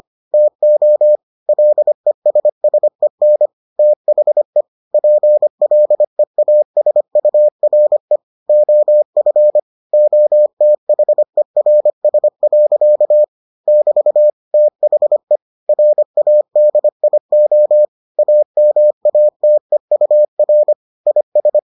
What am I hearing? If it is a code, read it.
Morse code, heard as a keyed tone, about 25 wpm, 600 Hz. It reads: TO LESSEN THE PLEASURE OF OTHERS. <BT> THE RADIO AMATEUR IS